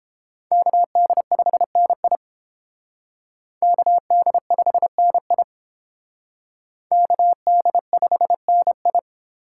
Morse code, heard as a keyed tone, about 29 wpm, 700 Hz. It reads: KD5NI KD5NI KD5NI